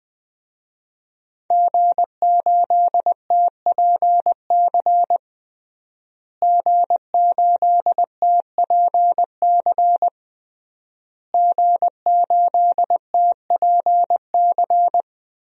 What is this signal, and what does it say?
Morse code, heard as a keyed tone, about 20 wpm, 700 Hz. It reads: G8TPC G8TPC G8TPC
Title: DXCC-100 Call Signs - 5 Characters - Repeated 3x - 20wpm